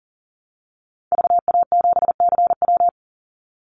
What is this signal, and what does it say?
Morse code, heard as a keyed tone, about 40 wpm, 700 Hz. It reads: VA7CW
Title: Call Signs - 40wpm